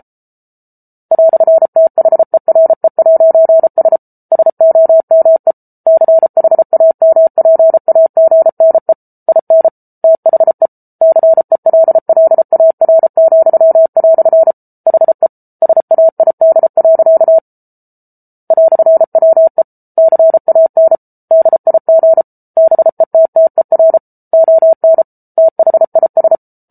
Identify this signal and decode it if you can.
Morse code, heard as a keyed tone, about 34 wpm, 650 Hz. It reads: "THERE'S SOME CHAMPAGNE IN THE CELLAR," HE SAID. "WE CAN DIG BETTER ON THIS